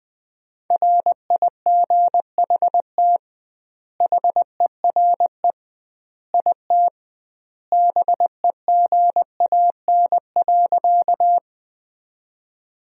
Morse code, heard as a keyed tone, about 20 wpm, 700 Hz. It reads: RIGHT HERE IT BEGAN.
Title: 2x Word-Spacing - Sentences from Top 300 Words - 20wpm